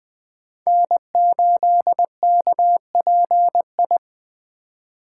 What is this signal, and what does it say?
Morse code, heard as a keyed tone, about 24 wpm, 700 Hz. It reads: N8KPI
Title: Speed Racing - USA 1x3 Call Signs 20wpm